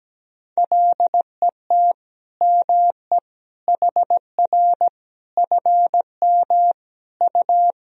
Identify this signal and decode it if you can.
Morse code, heard as a keyed tone, about 17 wpm, 700 Hz. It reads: LET ME HR FM U
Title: Sentences from Top 400 Words and Abbreviations 17wpm